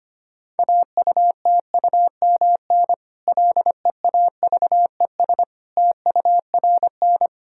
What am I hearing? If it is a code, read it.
Morse code, heard as a keyed tone, about 25 wpm, 700 Hz. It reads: AUTUMN LEAVES TURN
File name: practice / 25wpm / Difficult Sentences with Hints - 25wpm